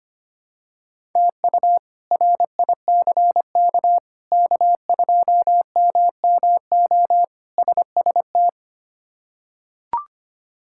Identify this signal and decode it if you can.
Morse code, heard as a keyed tone, about 25 wpm, 700 Hz. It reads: TU RICK K2MMO SST
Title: K1USN SST QSO Practice - 25wpm